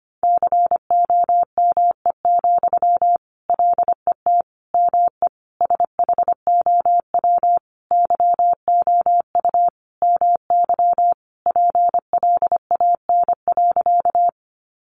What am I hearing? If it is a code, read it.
Morse code, heard as a keyed tone, about 25 wpm, 700 Hz. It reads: COME, LET ME SHOW YOU MY PLAN.
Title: Sentences from Top 500 Words 25wpm